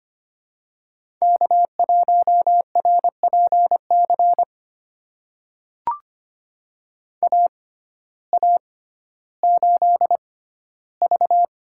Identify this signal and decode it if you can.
Morse code, heard as a keyed tone, about 25 wpm, spaced as for 14 wpm, 700 Hz. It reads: K1RPC AA8V
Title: USA Call Signs - 5 Characters - 6x Character Spacing - 25wpm